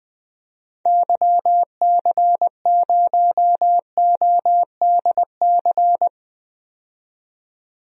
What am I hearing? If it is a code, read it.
Morse code, heard as a keyed tone, about 20 wpm, 700 Hz. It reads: YC0ODC
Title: DXCC-100 Call Signs - 6 Characters - 20wpm